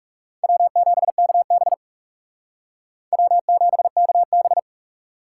Morse code, heard as a keyed tone, about 40 wpm, 700 Hz. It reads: W7KB W7KB